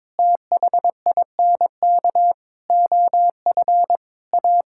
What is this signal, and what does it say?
Morse code, heard as a keyed tone, about 22 wpm, 700 Hz. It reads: THINK OF A
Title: Sentences from Top 1600 Words 22wpm